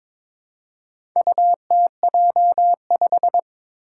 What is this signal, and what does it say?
Morse code, heard as a keyed tone, about 22 wpm, 700 Hz. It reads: UTJ5
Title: Four Challenging Letter-Number Combinations 22wpm